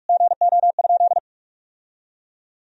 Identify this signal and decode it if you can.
Morse code, heard as a keyed tone, about 46 wpm, 700 Hz. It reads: GO?
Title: Sentences from Top 1000 Words - Review Part 4 - 45wpm